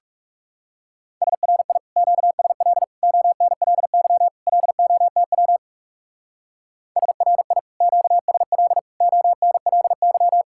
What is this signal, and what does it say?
Morse code, heard as a keyed tone, about 37 wpm, 700 Hz. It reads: SRI QSL ONLY LOTW SRI QSL ONLY
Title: Speed-Racing Common QSO Phrases - 30wpm